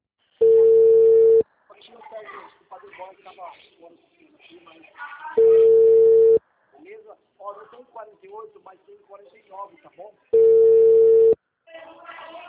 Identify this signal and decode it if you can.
Ringback tone (China, 450 Hz), with speech.
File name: out- -2011- - - .15320